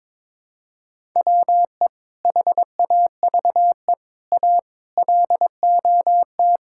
Morse code, heard as a keyed tone, about 22 wpm, 700 Hz. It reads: WE HAVE A LOT
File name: practice / 22wpm / Sentences with 8-Word Phrases - 22wpm